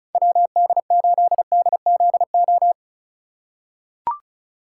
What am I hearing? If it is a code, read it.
Morse code, heard as a keyed tone, about 36 wpm, 700 Hz. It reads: WD8DZO
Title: USA Call Signs - 6 Characters - 4x Character Spacing - 35wpm